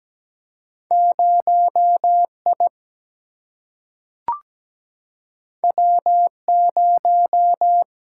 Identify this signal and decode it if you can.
Morse code, heard as a keyed tone, about 17 wpm, 700 Hz. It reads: 0I W0
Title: Single-Letter Single-Number 17wpm